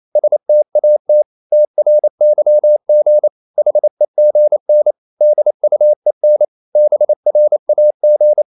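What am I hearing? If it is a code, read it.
Morse code, heard as a keyed tone, about 28 wpm, 600 Hz. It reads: STAT TRYG HEGN DUEN BRAG
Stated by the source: Danish short words sent in Morse at 28 wpm, from CW med Gnister.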